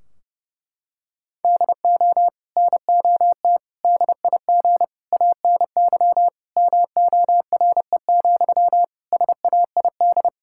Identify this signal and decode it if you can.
Morse code, heard as a keyed tone, about 30 wpm, 700 Hz. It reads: DO NOT DIG ANY MORE, SAID
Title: Sentences from Top 1600 Words 30wpm